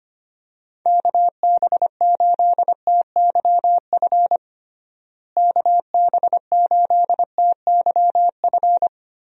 Morse code, heard as a keyed tone, about 25 wpm, 700 Hz. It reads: KB8TYF KB8TYF
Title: USA Call Signs - 6 Characters - Repeated 2x - 25wpm